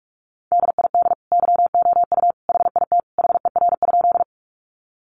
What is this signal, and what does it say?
Morse code, heard as a keyed tone, about 44 wpm, 700 Hz. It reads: DID YOU HIT HER?
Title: Sentences from Top 1200 Words 45wpm